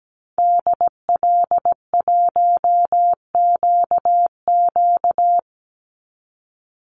Morse code, heard as a keyed tone, about 17 wpm, 700 Hz. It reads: DL1QQ
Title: Call Signs - 17wpm